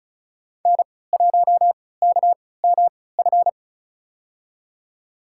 Morse code, heard as a keyed tone, about 36 wpm, spaced as for 20 wpm, 700 Hz. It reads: N1KMF
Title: USA Call Signs - 3x Character Spacing - 35wpm